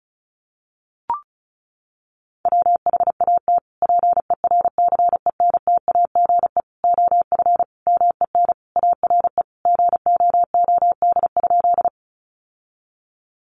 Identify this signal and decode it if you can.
Morse code, heard as a keyed tone, about 34 wpm, 700 Hz. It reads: WHAT PERCENTAGE OF MEN ARE GOOD?
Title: Sentences from Top 1300 Words 35wpm